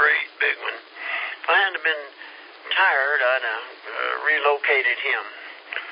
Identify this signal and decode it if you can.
Emergency Alert System (SAME) tones (header bursts), heard as harsh buzzing bursts, with speech. It is heard from 0.9 to 5.2 s.